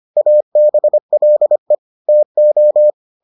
Morse code, heard as a keyed tone, about 25 wpm, 600 Hz. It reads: ABLE TO